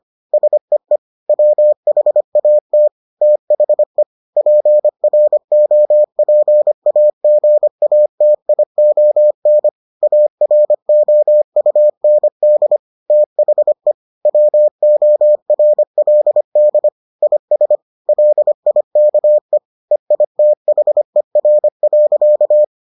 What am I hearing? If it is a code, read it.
Morse code, heard as a keyed tone, about 25 wpm, 600 Hz. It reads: SEE WHAT THE PROPAGATION AROUND THE WORLD IS LIKE EITHER.